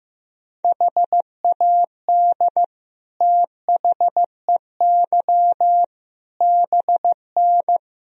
Morse code, heard as a keyed tone, about 15 wpm, 700 Hz. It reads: HAD THEY BN
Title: Sentences from Top 400 Words and Abbreviations 15wpm